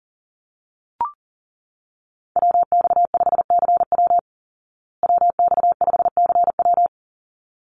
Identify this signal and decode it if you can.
Morse code, heard as a keyed tone, about 37 wpm, 700 Hz. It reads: WX5CW WX5CW
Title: Call Signs Repeated 3x - 40wpm